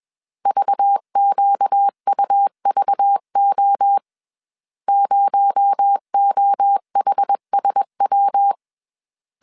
Morse code, heard as a keyed tone, about 21 wpm, 800 Hz. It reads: VQUVO 0OHSW
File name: CW / DET / 1 6 1500C S B 20wpm 2048